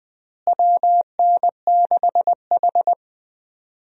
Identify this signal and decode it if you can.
Morse code, heard as a keyed tone, about 20 wpm, 700 Hz. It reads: WN6H
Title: USA Call Signs - 4 Characters - Repeated 2x - 20wpm